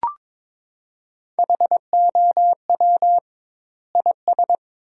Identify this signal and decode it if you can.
Morse code, heard as a keyed tone, about 22 wpm, 700 Hz. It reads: HOW IS
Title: 2x Word-Spacing - Sentences from Top 100 Words - 22wpm